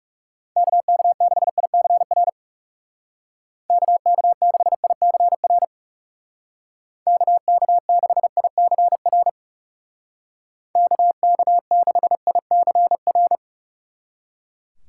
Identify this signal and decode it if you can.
Morse code, heard as a keyed tone, about 37 wpm, 700 Hz. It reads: KK6ICR KK6ICR KK6ICR KK6ICR
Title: Speed Racing - USA 2x3 Call Signs 30wpm